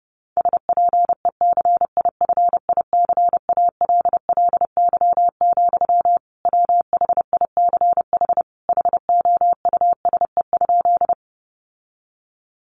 Morse code, heard as a keyed tone, about 30 wpm, 700 Hz. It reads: SPECIFICALLY, WHICH HOUSE?
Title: Sentences from Top 1100 Words 30wpm